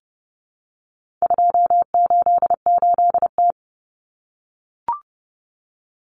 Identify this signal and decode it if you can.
Morse code, heard as a keyed tone, about 30 wpm, 700 Hz. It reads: 288T
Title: Common Cut Numbers - 4-Digit 30wpm